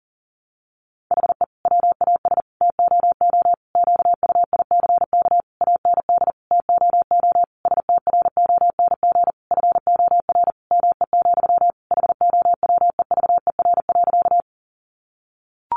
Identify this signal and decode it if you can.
Morse code, heard as a keyed tone, about 40 wpm, 700 Hz. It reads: HE WAS TOO QUICK AND TOO STRONG FOR ME, HOWEVER.
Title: Sentences from Top 600 Words 40wpm